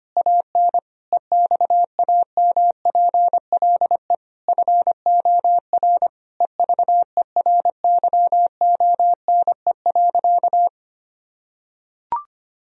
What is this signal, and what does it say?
Morse code, heard as a keyed tone, about 25 wpm, 700 Hz. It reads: AN EXAMPLE FOR EVERYONE.
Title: Sentences from Top 1000 Words - Review Part 1 - 25wpm